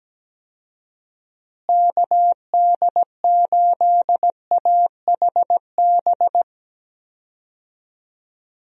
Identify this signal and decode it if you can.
Morse code, heard as a keyed tone, about 17 wpm, 700 Hz. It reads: KD8AHB
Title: USA Call Signs - 6 Characters - Repeated 2x - 17wpm